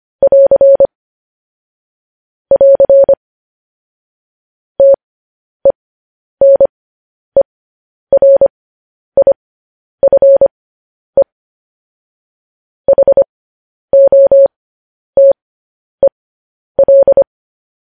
Morse code, heard as a keyed tone, about 25 wpm, spaced as for 10 wpm, 550 Hz. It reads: <AR> <AR> TENERIFE HOTEL
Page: news headlines in morse code at 10 wpm